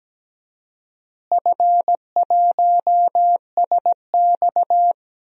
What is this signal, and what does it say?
Morse code, heard as a keyed tone, about 17 wpm, 700 Hz. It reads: F1SX
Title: DXCC-100 Call Signs - 4 Characters - Repeated 3x - 17wpm